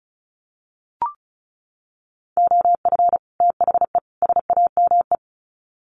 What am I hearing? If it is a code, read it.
Morse code, heard as a keyed tone, about 34 wpm, 700 Hz. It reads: OF THE SAME
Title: Sets of 3 Words 35wpm